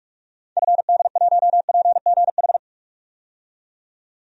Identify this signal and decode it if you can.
Morse code, heard as a keyed tone, about 46 wpm, 700 Hz. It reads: FD1PGH
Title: DXCC-100 Call Signs - 6 Characters - 45wpm